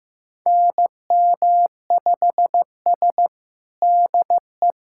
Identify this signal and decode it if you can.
Morse code, heard as a keyed tone, about 15 wpm, 700 Hz. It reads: NM5S DE
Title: USA Callsign DE Callsign - 4 Characters - 15wpm